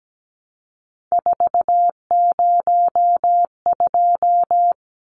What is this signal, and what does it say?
Morse code, heard as a keyed tone, about 17 wpm, 700 Hz. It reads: 402
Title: Numbers - 3-Digit 17wpm